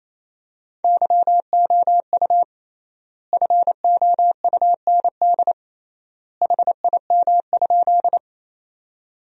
Morse code, heard as a keyed tone, about 28 wpm, 700 Hz. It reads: YOU FOUND HIM?
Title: 3x Word-Spacing - Sentences from Top 300 Words - 28wpm